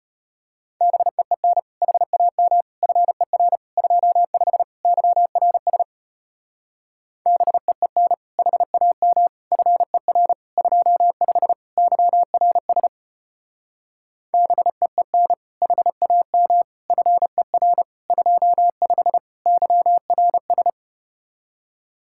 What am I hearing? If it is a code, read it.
Morse code, heard as a keyed tone, about 33 wpm, 700 Hz. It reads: BEEN HAM FER 25 YRS BEEN HAM FER 25 YRS BEEN HAM FER 25 YRS